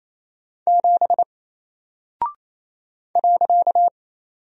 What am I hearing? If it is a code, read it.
Morse code, heard as a keyed tone, about 28 wpm, 700 Hz. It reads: 7 .
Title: Single Character (Letter:Number:Punctuation:Prosign) Alt 28wpm